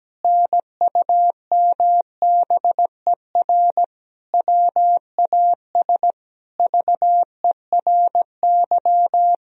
Morse code, heard as a keyed tone, about 17 wpm, 700 Hz. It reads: NUMBER WAS VERY
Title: Sentences from Top 1000 Words - Review Part 1 - 17wpm